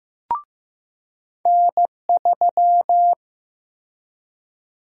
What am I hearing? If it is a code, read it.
Morse code, heard as a keyed tone, about 15 wpm, 700 Hz. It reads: N3
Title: USA Call Sign Prefixes - Repeated 2x - 15wpm